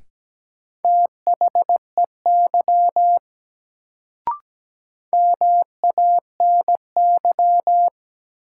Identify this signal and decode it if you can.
Morse code, heard as a keyed tone, about 17 wpm, 700 Hz. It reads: THEY MANY
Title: Sentences from Top 100 Words - New Word Drill - 17wpm